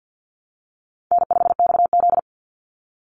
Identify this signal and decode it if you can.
Morse code, heard as a keyed tone, about 50 wpm, 700 Hz. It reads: N5XZ
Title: Call Signs Repeated 3x - 50wpm